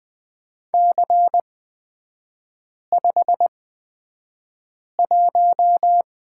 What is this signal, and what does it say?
Morse code, heard as a keyed tone, about 20 wpm, spaced as for 5 wpm, 690 Hz. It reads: C51
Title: 229 Lesson 37e - Letter x - Three Characters 20wpm